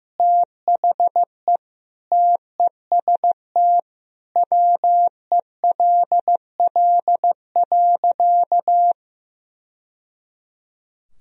Morse code, heard as a keyed tone, about 15 wpm, 700 Hz. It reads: THE TEST WELL.
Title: Sentences from Top 1000 Words - Review Part 3 - 15wpm